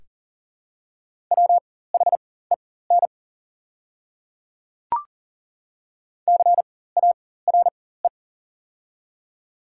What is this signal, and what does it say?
Morse code, heard as a keyed tone, about 40 wpm, spaced as for 19 wpm, 700 Hz. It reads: WHEN CARE
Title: 4x Character Spacing - Top 300 Words - 40wpm